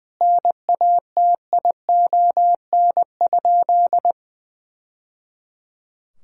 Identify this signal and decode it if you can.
Morse code, heard as a keyed tone, about 20 wpm, 700 Hz. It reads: NATION?
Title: Sentences from Top 1000 Words - Review Part 6 - 20wpm